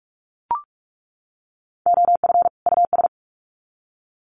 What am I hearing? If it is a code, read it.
Morse code, heard as a keyed tone, about 44 wpm, 700 Hz. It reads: OF US
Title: Sets of 2 Words 45wpm